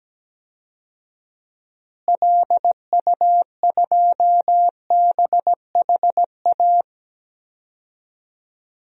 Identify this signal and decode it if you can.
Morse code, heard as a keyed tone, about 17 wpm, 700 Hz. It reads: LU2BHA
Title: DXCC-100 Call Signs - 6 Characters - Repeated 2x - 17wpm